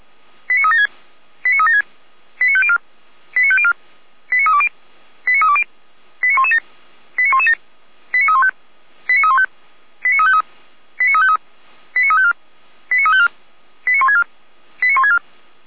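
Five-tone selective call (ZVEI), heard as quick runs of five tones, tones of about 70 ms. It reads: 89376 89376 89504 89504 89320 89320 89117 89117 89315 89315 89353 89353 89354 89354 89164 89164
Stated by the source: ZVEI five-tone selective call received on an SDR.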